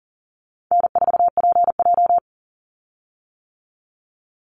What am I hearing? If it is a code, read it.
Morse code, heard as a keyed tone, about 40 wpm, 700 Hz. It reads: N4PJ
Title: Call Signs - 40wpm